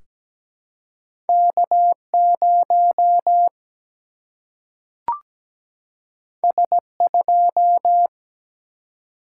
Morse code, heard as a keyed tone, about 17 wpm, 700 Hz. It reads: K 0 S 2
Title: Single-Letter Single-Number 17wpm